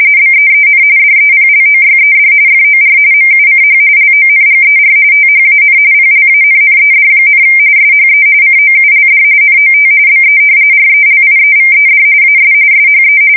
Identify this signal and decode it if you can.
RTTY, about 45.45 baud. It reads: BEEN DEPLOYED FOR MUCH LONGER THAN PLANNED, SHE COULD NOT BE RELIEVED AS SHE
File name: fsk170 / battleshipnj